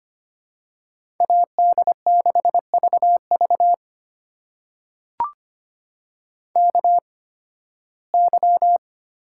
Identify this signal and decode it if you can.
Morse code, heard as a keyed tone, about 25 wpm, spaced as for 16 wpm, 700 Hz. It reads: AD6VV KY
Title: USA Call Signs - 8x Character Spacing - 25wpm